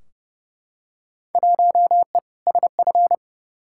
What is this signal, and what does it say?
Morse code, heard as a keyed tone, about 30 wpm, 700 Hz. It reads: 1E SF
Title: Field Day Exchanges - 30wpm